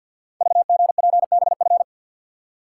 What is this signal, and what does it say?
Morse code, heard as a keyed tone, about 49 wpm, 700 Hz. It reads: VGPBF
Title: Code Groups - 5 Letters - 50wpm